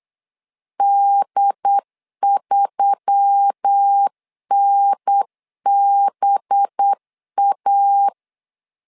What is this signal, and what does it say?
Morse code, heard as a keyed tone, about 8 wpm, 800 Hz. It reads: D3NBA
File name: LEZIONI / 5 LBVC / 5 LBV 8wpm 2837